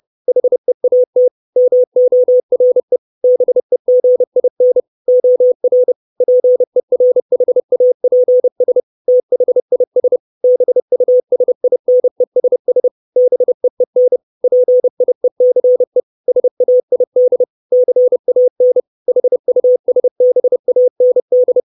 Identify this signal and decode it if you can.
Morse code, heard as a keyed tone, about 30 wpm, 500 Hz. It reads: HEAT MORE BEGIN OR PERHAPS THIS BUSINESS BEEN PIECE SAID CAN HUSBAND
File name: 632 Words 30WPM